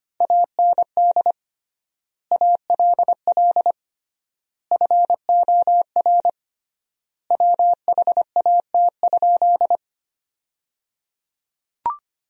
Morse code, heard as a keyed tone, about 25 wpm, 700 Hz. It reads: AND ALL FOR WHAT?
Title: 3x Word-Spacing - Sentences from Top 100 Words - 25wpm